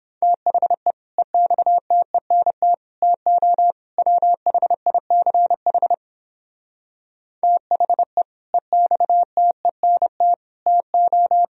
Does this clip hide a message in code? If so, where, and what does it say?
Yes: Morse code from 0.2 to 11.4 s, heard as a keyed tone, about 29 wpm, 700 Hz. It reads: THE EXTENT TO WHICH THE EXTENT TO